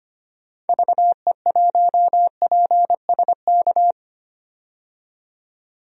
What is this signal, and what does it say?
Morse code, heard as a keyed tone, about 25 wpm, 700 Hz. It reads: VE1PSK
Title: DXCC-100 Call Signs - 6 Characters - 25wpm